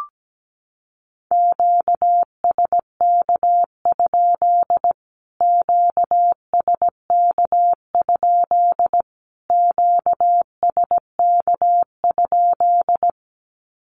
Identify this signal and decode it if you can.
Morse code, heard as a keyed tone, about 17 wpm, 700 Hz. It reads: QSK? QSK? QSK?
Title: Q-Codes Repeated 3x 17wpm